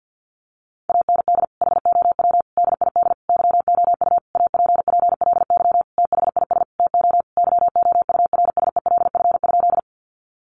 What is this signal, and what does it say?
Morse code, heard as a keyed tone, about 50 wpm, 700 Hz. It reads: AND HOW DID YOU APPLY THIS TO YOURSELF?
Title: Sentences from Top 1200 Words 50wpm